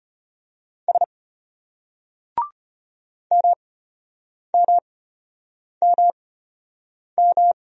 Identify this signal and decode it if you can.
Morse code, heard as a keyed tone, about 33 wpm, spaced as for 8 wpm, 700 Hz. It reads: S MMMM